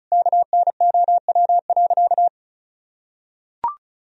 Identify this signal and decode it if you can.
Morse code, heard as a keyed tone, about 34 wpm, 700 Hz. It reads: KNOW.
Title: Sentences from Top 1000 Words - Review Part 2 - 35wpm